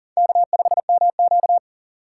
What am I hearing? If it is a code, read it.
Morse code, heard as a keyed tone, about 39 wpm, 700 Hz. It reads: K5MQ
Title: USA Call Signs - 4 Characters - 6x Character Spacing - 40wpm